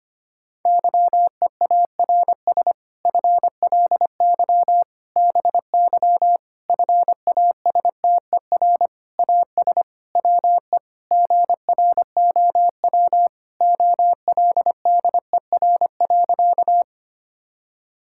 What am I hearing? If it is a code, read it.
Morse code, heard as a keyed tone, about 25 wpm, 700 Hz. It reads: YEARS FLY BY FASTER AS WE GROW OLDER.
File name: practice / 25wpm / Difficult Sentences with Hints - 25wpm